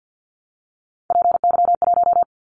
Morse code, heard as a keyed tone, about 54 wpm, 700 Hz. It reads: PY1
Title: Callsign Prefixes for Top 100 DXCC - 50wpm